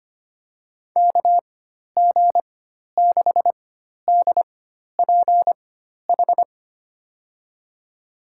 Morse code, heard as a keyed tone, about 25 wpm, spaced as for 12 wpm, 700 Hz. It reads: KG6DPH
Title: USA Call Signs - 6 Characters - 4x Character Spacing - 25wpm